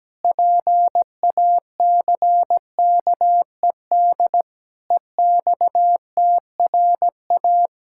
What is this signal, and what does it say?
Morse code, heard as a keyed tone, about 17 wpm, 700 Hz. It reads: PACKED EXTRA
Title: Pangram Challenge v2 - 17wpm